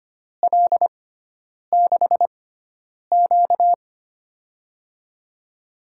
Morse code, heard as a keyed tone, about 25 wpm, spaced as for 9 wpm, 700 Hz. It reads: L6Q
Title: USA Call Signs - 4 Characters - 6x Character Spacing - 25wpm